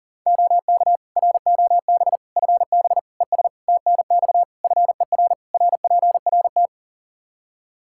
Morse code, heard as a keyed tone, about 37 wpm, 700 Hz. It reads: OK ROB FB ES TNX FER RPRT
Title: Practice QSO - Exchange 2 - Info - 40wpm